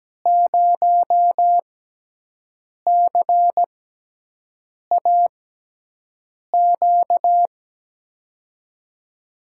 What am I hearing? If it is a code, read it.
Morse code, heard as a keyed tone, about 17 wpm, spaced as for 6 wpm, 700 Hz. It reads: 0CAQ
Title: USA Call Signs - 6 Characters - 6x Character Spacing - 17wpm